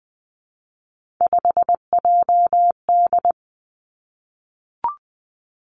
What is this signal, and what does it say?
Morse code, heard as a keyed tone, about 19 wpm, 700 Hz. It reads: 5JD
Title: Three Challenging Letter-Number Combinations 20wpm